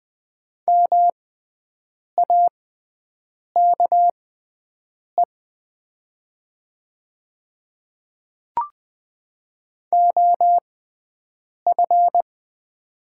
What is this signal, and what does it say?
Morse code, heard as a keyed tone, about 20 wpm, spaced as for 7 wpm, 700 Hz. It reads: MAKE OF
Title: 6x Character Spacing - Top 100 Words - 20wpm